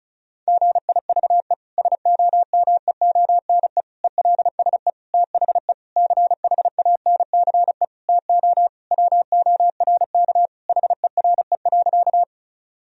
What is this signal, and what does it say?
Morse code, heard as a keyed tone, about 34 wpm, 700 Hz. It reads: GIVE SOMEONE ELSE THE CHANCE TO WORK HERE.
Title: Sentences from Top 1000 Words - Review Part 4 - 35wpm